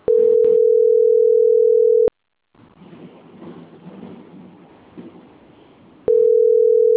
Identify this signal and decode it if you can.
Ringback tone (North America), 440 + 480 Hz.